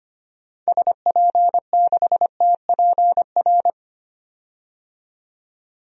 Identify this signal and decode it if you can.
Morse code, heard as a keyed tone, about 25 wpm, 700 Hz. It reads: SP6TPR
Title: DXCC-100 Call Signs - 6 Characters - 25wpm